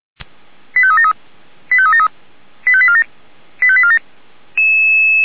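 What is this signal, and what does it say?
Five-tone selective call (ZVEI), heard as quick runs of five tones, tones of about 70 ms. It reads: 85373 85373 85847 85847
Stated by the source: Five-tone ZVEI selective call 89484, received by SDR.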